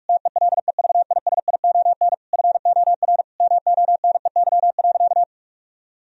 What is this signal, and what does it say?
Morse code, heard as a keyed tone, about 41 wpm, 700 Hz. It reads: TELEVISION FOR MONEY.
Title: Sentences from Top 1000 Words - Review Part 4 - 45wpm